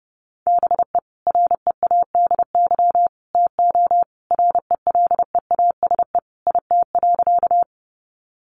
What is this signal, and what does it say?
Morse code, heard as a keyed tone, about 30 wpm, 700 Hz. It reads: BE READY TO RELEASE IT.
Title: Sentences from Top 1100 Words 30wpm